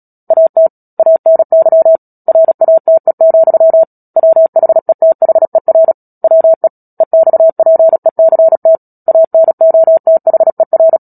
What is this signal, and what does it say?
Morse code, heard as a keyed tone, about 37 wpm, 650 Hz. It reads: AT ANY RATE, WHETHER WE EXPECT ANOTHER
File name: Book Two / 10-Chapter 10 - Epilogue - 36 WPM